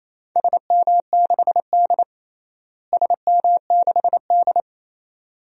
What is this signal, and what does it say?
Morse code, heard as a keyed tone, about 28 wpm, 700 Hz. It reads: SM6D SM6D
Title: DXCC-100 Call Signs - 4 Characters - Repeated 2x - 28wpm